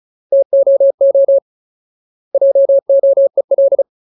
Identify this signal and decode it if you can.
Morse code, heard as a keyed tone, about 34 wpm, 550 Hz. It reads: TOO JOEL
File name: Ultimate Word List / 35WPM Ultimate